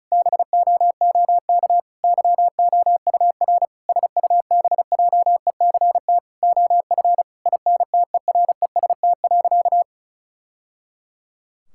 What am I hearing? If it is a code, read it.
Morse code, heard as a keyed tone, about 36 wpm, 700 Hz. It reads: BOOK YOUR SUBJECT OF INTEREST.
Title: Sentences from Top 1000 Words - Review Part 5 - 35wpm